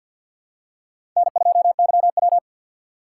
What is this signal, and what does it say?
Morse code, heard as a keyed tone, about 52 wpm, 700 Hz. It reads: N2YW
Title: USA Call Signs - 2x Character Spacing - 50wpm